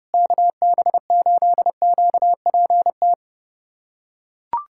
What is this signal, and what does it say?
Morse code, heard as a keyed tone, about 30 wpm, 700 Hz. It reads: KB8QPT
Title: USA Callsign DE Callsign - 6 Characters - 30wpm